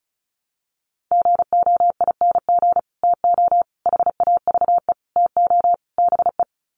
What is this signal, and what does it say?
Morse code, heard as a keyed tone, about 34 wpm, 700 Hz. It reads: GOING TO HAVE TO BE
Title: Sets of 5 Words 35wpm